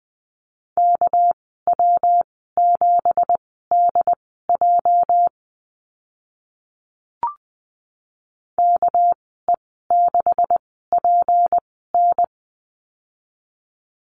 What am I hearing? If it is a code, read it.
Morse code, heard as a keyed tone, about 20 wpm, spaced as for 14 wpm, 700 Hz. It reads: KW7DJ KE6PN
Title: 213 Lesson 34g - Number 8 - Call Signs 20wpm